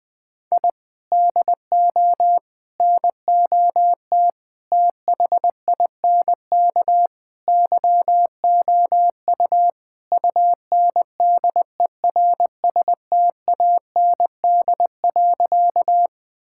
Morse code, heard as a keyed tone, about 20 wpm, 700 Hz. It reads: I DO NOT THINK YOU UNDERSTAND.
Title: Sentences with 4-Word Phrases - 20wpm